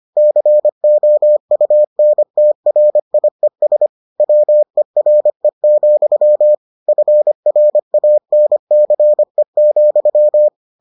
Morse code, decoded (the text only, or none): COUNTRIES WERE, FRANCE,